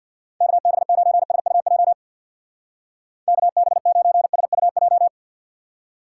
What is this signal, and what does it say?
Morse code, heard as a keyed tone, about 52 wpm, 700 Hz. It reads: KB9SUJ KB9SUJ
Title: Speed Racing - USA 2x3 Call Signs 50wpm